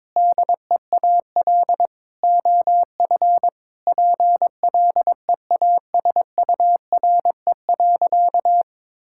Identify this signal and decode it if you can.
Morse code, heard as a keyed tone, about 22 wpm, 700 Hz. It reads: DEAL OF PLEASURE.
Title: Sentences from Top 1600 Words 22wpm